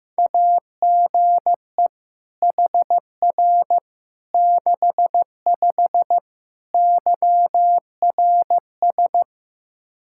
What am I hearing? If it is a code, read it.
Morse code, heard as a keyed tone, about 15 wpm, 700 Hz. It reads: AGE HR 65 YRS
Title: Practice QSO - Exchange 3 - More Info - 15wpm